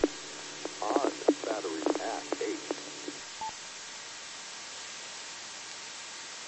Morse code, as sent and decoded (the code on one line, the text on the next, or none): none